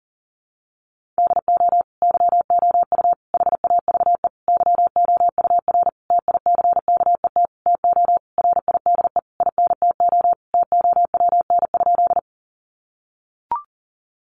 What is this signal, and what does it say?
Morse code, heard as a keyed tone, about 40 wpm, 700 Hz. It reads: DO YOU HAVE YOUR TICKET TO RIDE INTO TOWN?
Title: Sentences from Top 1400 Words 40wpm